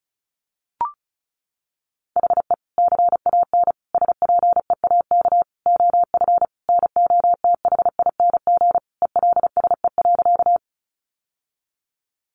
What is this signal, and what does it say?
Morse code, heard as a keyed tone, about 34 wpm, 700 Hz. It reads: HE CAN SPEAK OF NOTHING ELSE.
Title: Sentences from Top 1100 Words 35wpm